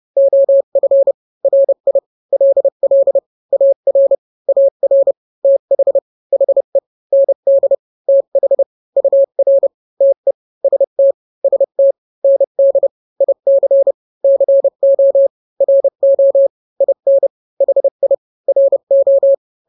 Morse code, heard as a keyed tone, about 30 wpm, 550 Hz. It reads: OF RI LL AR AR TH HE ND TH UR TE ST ST ND IC CO RO IN HI RO
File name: n-grams / 2-grams 30 25 WPM